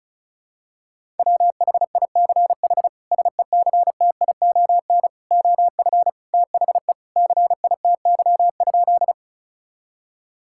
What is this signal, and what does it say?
Morse code, heard as a keyed tone, about 36 wpm, 700 Hz. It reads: WHICH SECTION OF THE CITY?
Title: Sentences from Top 1000 Words - Review Part 5 - 35wpm